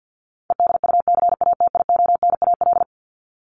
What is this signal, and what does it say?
Morse code, heard as a keyed tone, about 50 wpm, 700 Hz. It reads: EDUCATIONAL
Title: Top 1001-1250 Words 50wpm